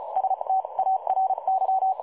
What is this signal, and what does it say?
Morse code, heard as a keyed tone, about 33 wpm, spaced as for 29 wpm, 760 Hz. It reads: STARY
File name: dlazaawansowanych / nagraniazpasma / websdr recording 3565.0kHz